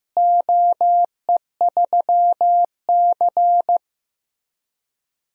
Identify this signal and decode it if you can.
Morse code, heard as a keyed tone, about 16 wpm, 700 Hz. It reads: OE3C
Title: DXCC-100 Call Signs - 4 Characters - Repeated 3x - 15wpm